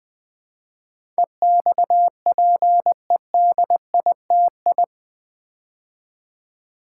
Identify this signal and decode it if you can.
Morse code, heard as a keyed tone, about 20 wpm, 700 Hz. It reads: EXPEDITI
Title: Build A Word - 4 Letters At A Time - 10-Letter Words - 20wpm